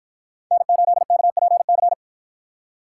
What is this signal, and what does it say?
Morse code, heard as a keyed tone, about 53 wpm, 700 Hz. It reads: N8KPC